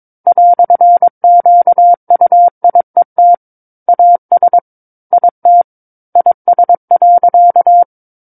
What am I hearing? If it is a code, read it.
Morse code, heard as a keyed tone, about 22 wpm, 700 Hz. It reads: "QUIET AS IT IS.
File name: Book One / 13-Chapter 13 - 22 WPM